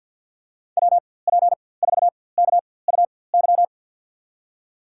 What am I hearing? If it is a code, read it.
Morse code, heard as a keyed tone, about 47 wpm, spaced as for 23 wpm, 700 Hz. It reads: WP4KUY
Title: USA Call Signs - 6 Characters - 4x Character Spacing - 50wpm